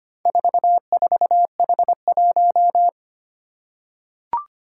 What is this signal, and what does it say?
Morse code, heard as a keyed tone, about 25 wpm, 700 Hz. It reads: 44H1
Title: Four Challenging Letter-Number Combinations 25wpm